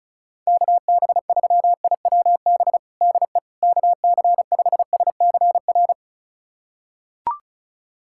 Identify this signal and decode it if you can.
Morse code, heard as a keyed tone, about 34 wpm, 690 Hz. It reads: KB3IWB DE KC5SCR
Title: USA Callsign DE Callsign - 6 Characters - 35wpm